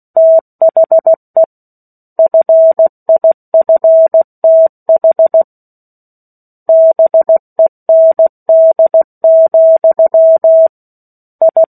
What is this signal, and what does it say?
Morse code, heard as a keyed tone, about 16 wpm, 650 Hz. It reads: THE FIFTH BEND, I
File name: 03-Chapter-03 16 WPM